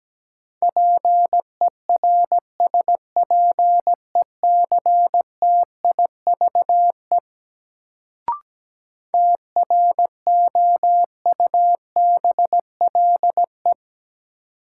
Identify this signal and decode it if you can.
Morse code, heard as a keyed tone, about 17 wpm, 700 Hz. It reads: PERSPECTIVE TROUBLE
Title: Sentences from Top 1200 Words - New Word Drill - 17wpm